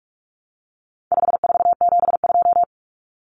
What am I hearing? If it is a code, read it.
Morse code, heard as a keyed tone, about 44 wpm, 700 Hz. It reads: 5472
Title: Common Cut Numbers - 4-Digit 45wpm